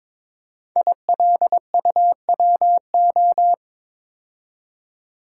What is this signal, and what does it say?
Morse code, heard as a keyed tone, about 23 wpm, 700 Hz. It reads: ILUWO
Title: Code Groups - 5 Letters - 22wpm